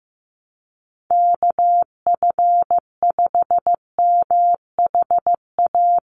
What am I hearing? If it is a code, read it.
Morse code, heard as a keyed tone, about 15 wpm, 700 Hz. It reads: KF5MHA
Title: USA Call Signs 15wpm